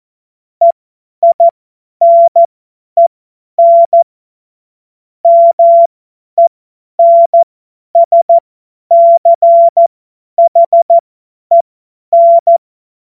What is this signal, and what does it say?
Morse code, heard as a keyed tone, about 14 wpm, spaced as for 10 wpm, 680 Hz. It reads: EINEN MENSCHEN
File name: Morsekurs / Witze / Witz 05. Oktober 2021, 50 F70